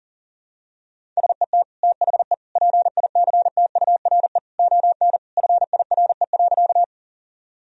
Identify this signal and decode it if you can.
Morse code, heard as a keyed tone, about 40 wpm, 700 Hz. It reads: SET THE PICTURE ON FIRE.
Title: Sentences from Top 1000 Words - Review Part 1 - 40wpm